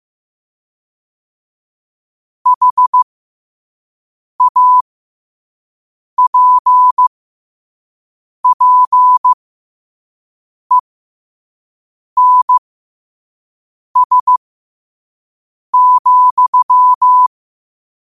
Morse code, heard as a keyed tone, about 15 wpm, spaced as for 5 wpm, 1000 Hz. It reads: HAPPENS,
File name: qotdfiles / 5wpm